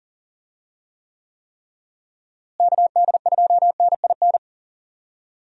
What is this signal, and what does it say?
Morse code, heard as a keyed tone, about 40 wpm, 700 Hz. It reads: KD2NIN
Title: USA Call Signs - 6 Characters - Repeated 3x - 40wpm